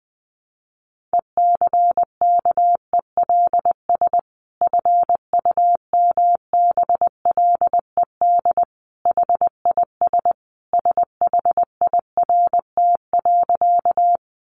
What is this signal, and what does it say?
Morse code, heard as a keyed tone, about 20 wpm, 700 Hz. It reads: ECKELS FUMBLED HIS SHIRT.
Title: A Sound Of Thunder 20wpm